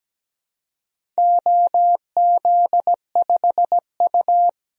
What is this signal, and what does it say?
Morse code, heard as a keyed tone, about 17 wpm, 700 Hz. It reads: OZ5U
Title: DXCC-100 Call Signs - 4 Characters - Repeated 3x - 17wpm